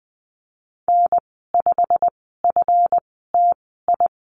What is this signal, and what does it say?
Morse code, heard as a keyed tone, about 20 wpm, spaced as for 14 wpm, 700 Hz. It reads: N5FTI
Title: 140 Lesson 23g - Letter f - Call Signs 20wpm